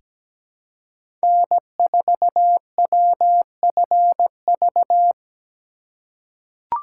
Morse code, heard as a keyed tone, about 17 wpm, 700 Hz. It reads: N4WFV
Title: USA Call Signs - 5 Characters - 3x Character Spacing - 17wpm